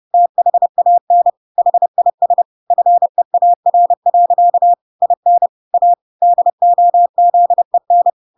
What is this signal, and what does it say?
Morse code, heard as a keyed tone, about 30 wpm, 700 Hz. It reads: THAN HIS FEAR. IN A DOZEN
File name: Pinocchio / Pinocchio ch6 30 30 ik1hge